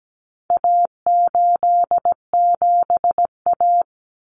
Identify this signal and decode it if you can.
Morse code, heard as a keyed tone, about 17 wpm, 700 Hz. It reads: A87A
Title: Common Cut Numbers - 4-Digit 17wpm